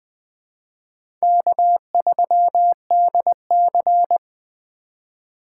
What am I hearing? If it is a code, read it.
Morse code, heard as a keyed tone, about 20 wpm, 700 Hz. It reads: K3DC
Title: USA Call Signs - 3x Character Spacing - 20wpm